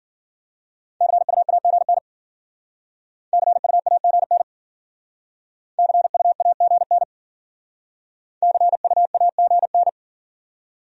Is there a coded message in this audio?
Yes: Morse code from 1.0 to 9.9 s, heard as a keyed tone, about 51 wpm, 700 Hz. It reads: CUAGN CUAGN CUAGN CUAGN